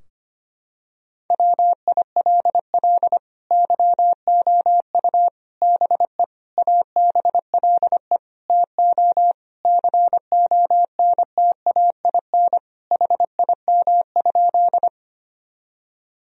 Morse code, heard as a keyed tone, about 25 wpm, 700 Hz. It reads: WILL YOU BE ABLE TO CONTAIN HIM?
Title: Sentences from Top 1000 Words - Review Part 2 - 25wpm